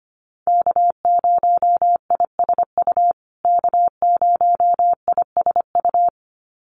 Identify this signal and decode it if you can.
Morse code, heard as a keyed tone, about 25 wpm, 700 Hz. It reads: K0ISU K0ISU
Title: USA Call Signs Repeated 2x 25wpm